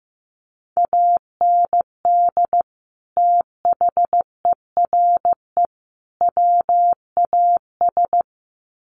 Morse code, heard as a keyed tone, about 15 wpm, 700 Hz. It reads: AND THERE WAS
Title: Sets of 3 Words 15wpm